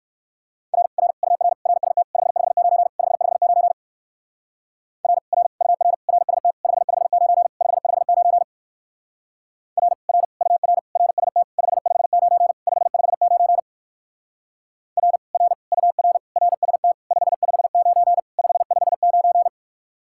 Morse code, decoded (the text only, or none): R R UR RST 559 559 R R UR RST 559 559 R R UR RST 559 559 R R UR RST 559 559